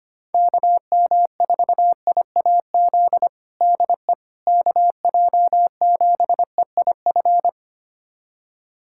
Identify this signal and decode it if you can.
Morse code, heard as a keyed tone, about 25 wpm, 700 Hz. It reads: KM4IAZ DE KJ7EIF
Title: USA Callsign DE Callsign 25wpm